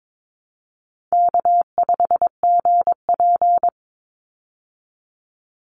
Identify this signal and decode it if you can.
Morse code, heard as a keyed tone, about 22 wpm, 700 Hz. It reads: K5GP
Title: Call Signs Repeated 2x - 22wpm